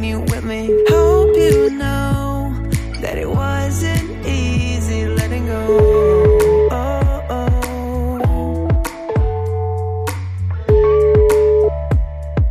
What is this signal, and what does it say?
Ringback tone (Europe, 425 Hz), with noise.